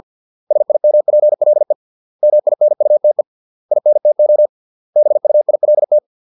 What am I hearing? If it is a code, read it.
Morse code, heard as a keyed tone, about 50 wpm, 600 Hz. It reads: SIMPLE MINUTE INTO BUILT